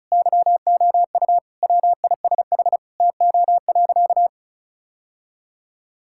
Morse code, heard as a keyed tone, about 35 wpm, 700 Hz. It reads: YOU WISH TO.
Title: Sentences from Top 1600 Words 35wpm